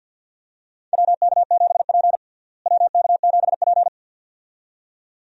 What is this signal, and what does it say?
Morse code, heard as a keyed tone, about 53 wpm, 690 Hz. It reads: WK7P WK7P
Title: USA Call Signs - 4 Characters - Repeated 2x - 50wpm